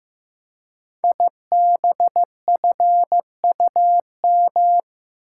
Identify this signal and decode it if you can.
Morse code, heard as a keyed tone, about 15 wpm, 700 Hz. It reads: IBFUM
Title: Code Groups - 5 Letter:Numbers - 15wpm